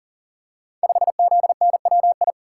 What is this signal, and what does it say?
Morse code, heard as a keyed tone, about 40 wpm, 700 Hz. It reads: 5ZNWI
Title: Code Groups - 5 Letter:Numbers - 40wpm